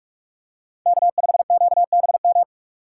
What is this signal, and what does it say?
Morse code, heard as a keyed tone, about 46 wpm, 700 Hz. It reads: K5QBM
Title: USA Call Signs - 5 Characters - 8x Character Spacing - 45wpm